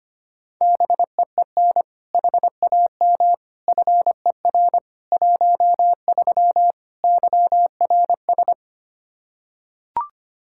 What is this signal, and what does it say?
Morse code, heard as a keyed tone, about 25 wpm, 700 Hz. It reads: BEEN HAM FER 13 YRS
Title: Practice QSO - Exchange 3 - More Info - 25wpm